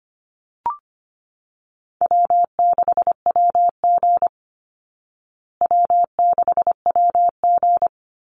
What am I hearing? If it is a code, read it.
Morse code, heard as a keyed tone, about 25 wpm, 700 Hz. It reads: W6WG W6WG
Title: Call Signs Repeated 3x - 25wpm